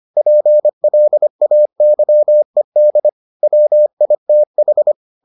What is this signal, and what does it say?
Morse code, heard as a keyed tone, about 25 wpm, 600 Hz. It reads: PLAYED WITH